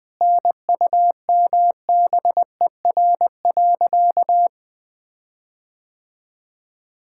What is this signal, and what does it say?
Morse code, heard as a keyed tone, about 21 wpm, 700 Hz. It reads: NUMBER.
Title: Sentences with 5-Word Phrases - Spoken Beginning - 20wpm